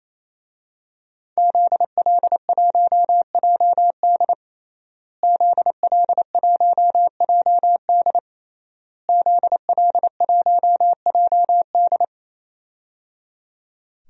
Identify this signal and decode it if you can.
Morse code, heard as a keyed tone, about 28 wpm, 700 Hz. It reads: ZL1JD ZL1JD ZL1JD
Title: DXCC-100 Call Signs - 5 Characters - Repeated 3x - 28wpm